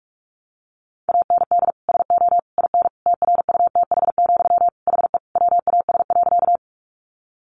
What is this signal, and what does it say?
Morse code, heard as a keyed tone, about 44 wpm, 700 Hz. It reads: AND SO IN TRUTH, HE WAS.
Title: Sentences from Top 800 Words 45wpm